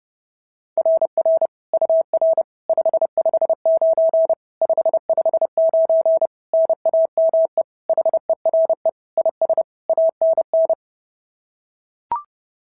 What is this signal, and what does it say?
Morse code, heard as a keyed tone, about 30 wpm, 650 Hz. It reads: RR UR 559 559 NAME HERE IS ANN
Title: SOTA QSO Practice 30wpm